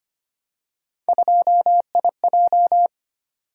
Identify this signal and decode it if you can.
Morse code, heard as a keyed tone, about 25 wpm, 700 Hz. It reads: 2IJ
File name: Three Letter-Number Combinations - 25wpm